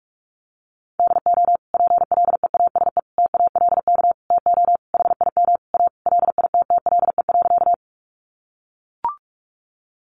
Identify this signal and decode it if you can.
Morse code, heard as a keyed tone, about 44 wpm, 700 Hz. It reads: DO PLEASE TALK TO HIM A LITTLE.
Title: Sentences from Top 1500 Words 45wpm